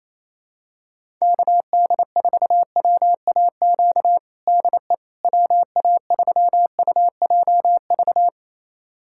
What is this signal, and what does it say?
Morse code, heard as a keyed tone, about 28 wpm, 700 Hz. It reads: KD4WAQ DE WA3UJV
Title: USA Callsign DE Callsign 28wpm